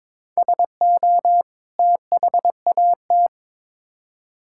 Morse code, heard as a keyed tone, about 22 wpm, 700 Hz. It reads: SO THAT
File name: Build A Phrase - 2-Word Phrases - 22wpm